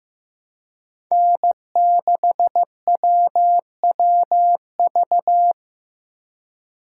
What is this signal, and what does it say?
Morse code, heard as a keyed tone, about 15 wpm, 700 Hz. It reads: N6WWV
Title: USA Call Signs - 5 Characters - 2x Character Spacing - 15wpm